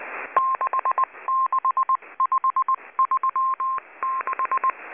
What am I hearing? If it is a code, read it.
Morse code, heard as a keyed tone, about 19 wpm, 1060 Hz. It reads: 66536
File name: HM02- -0540utc